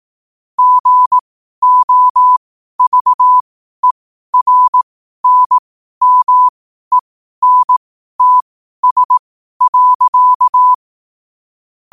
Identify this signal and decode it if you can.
Morse code, heard as a keyed tone, about 18 wpm, spaced as for 12 wpm, 1000 Hz. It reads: GOVERNMENTS.
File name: qotdfiles / 12wpm